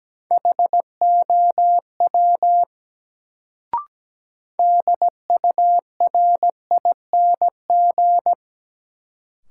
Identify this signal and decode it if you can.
Morse code, heard as a keyed tone, about 17 wpm, 700 Hz. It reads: HOW DURING
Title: Sets of 2 Words - New Word Drill - 17wpm